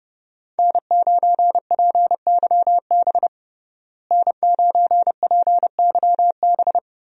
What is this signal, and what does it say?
Morse code, heard as a keyed tone, about 30 wpm, 700 Hz. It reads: N9PYB N9PYB
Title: USA Call Signs - 5 Characters - Repeated 3x - 30wpm